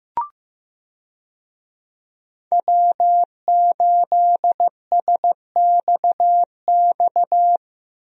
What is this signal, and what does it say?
Morse code, heard as a keyed tone, about 16 wpm, 700 Hz. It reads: W8SXX
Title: USA Call Signs - 5 Characters - Repeated 3x - 15wpm